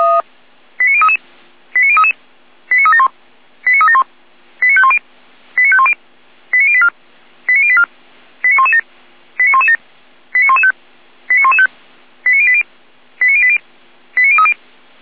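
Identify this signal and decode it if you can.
Five-tone selective call (ZVEI), heard as quick runs of five tones, tones of about 80 ms. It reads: 89033 89033 89371 89371 89510 89510 89084 89084 89117 89117 89115 89115 89080 89080 89030
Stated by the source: ZVEI five-tone selective call 89094 received on SDR.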